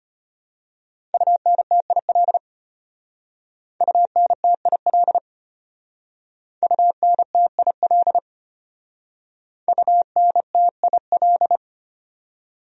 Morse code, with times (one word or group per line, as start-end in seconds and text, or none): 1.1-2.4 UNTIL
3.8-5.2 UNTIL
6.6-8.2 UNTIL
9.7-11.6 UNTIL